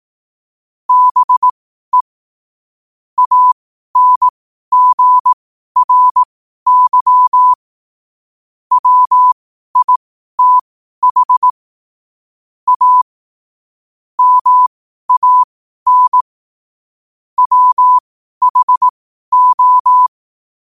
Morse code, heard as a keyed tone, about 18 wpm, spaced as for 12 wpm, 1000 Hz. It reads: BE ANGRY WITH A MAN WHO
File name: qotdfiles / 12wpm